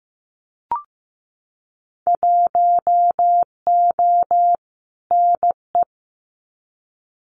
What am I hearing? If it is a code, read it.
Morse code, heard as a keyed tone, about 15 wpm, 700 Hz. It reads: 1O NE
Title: Winter Field Day Exchanges Encore 15wpm